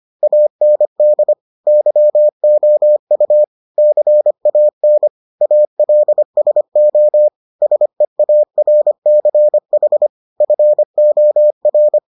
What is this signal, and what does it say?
Morse code, heard as a keyed tone, about 25 wpm, 600 Hz. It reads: AND YOU CAN ALSO SEARCH FOR